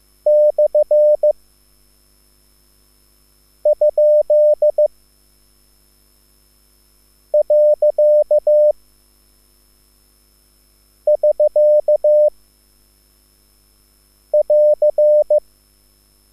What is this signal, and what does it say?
Morse code, heard as a keyed tone, about 15 wpm, 600 Hz. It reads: / ? . <SK> <AR>